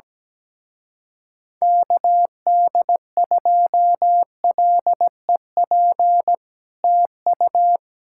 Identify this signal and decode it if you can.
Morse code, heard as a keyed tone, about 17 wpm, 700 Hz. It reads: KD2LEP TU
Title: Speed-Racing Common QSO Phrases - 17wpm